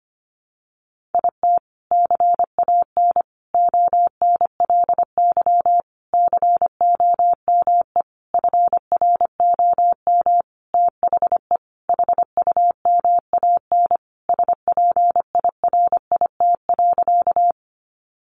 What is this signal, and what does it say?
Morse code, heard as a keyed tone, about 25 wpm, 700 Hz. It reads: IT CAN ONLY COME FROM THE HUMAN SPIRIT.
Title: Sentences from Top 1400 Words 25wpm